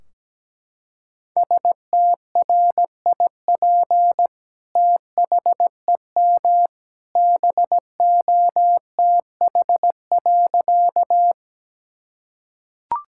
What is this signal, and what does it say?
Morse code, heard as a keyed tone, about 17 wpm, 700 Hz. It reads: STRIP THEM BOTH.
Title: Sentences from Top 1600 Words 17wpm